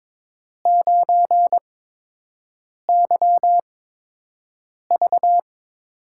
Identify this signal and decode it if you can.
Morse code, heard as a keyed tone, about 22 wpm, spaced as for 6 wpm, 700 Hz. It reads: 9YV
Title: USA Call Signs - 4 Characters - 8x Character Spacing - 22wpm